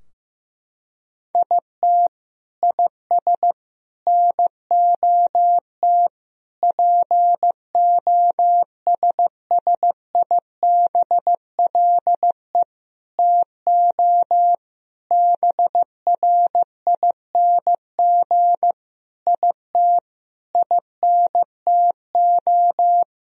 Morse code, heard as a keyed tone, about 15 wpm, 700 Hz. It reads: IT IS NOT POSSIBLE TO BRING IT INTO
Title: Sentences from Top 1000 Words - Review Part 4 - 15wpm